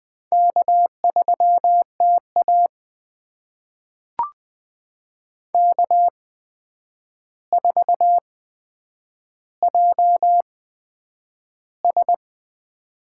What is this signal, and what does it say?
Morse code, heard as a keyed tone, about 20 wpm, spaced as for 9 wpm, 700 Hz. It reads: K3TA K4JS
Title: USA Call Signs - 4 Characters - 8x Character Spacing - 20wpm